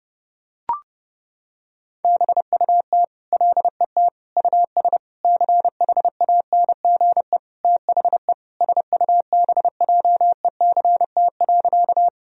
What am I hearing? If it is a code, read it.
Morse code, heard as a keyed tone, about 30 wpm, 700 Hz. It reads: BUT LET US CHANGE THE SUBJECT.
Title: Sentences from Top 1000 Words - Review Part 1 - 30wpm